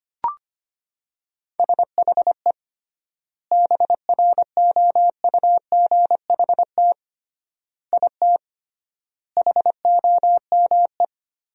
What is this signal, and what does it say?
Morse code, heard as a keyed tone, about 25 wpm, 700 Hz. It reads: SHE BROUGHT IT HOME
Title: 3x Word-Spacing - Sentences from Top 500 Words - 25wpm